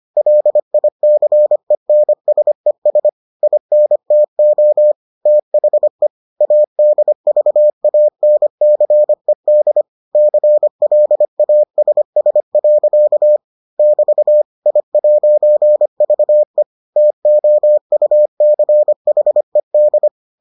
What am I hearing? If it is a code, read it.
Morse code, heard as a keyed tone, about 25 wpm, 600 Hz. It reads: LICENSES INTO THE ADVANCED CLASS. <BT> I'VE TOUCHED